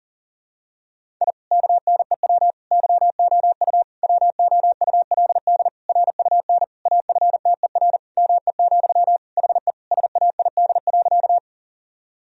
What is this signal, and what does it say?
Morse code, heard as a keyed tone, about 40 wpm, 700 Hz. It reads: I KNEW YOU WOULD RUN AFTER ME, HE SAID.
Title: Sentences from Top 1000 Words - Review Part 1 - 40wpm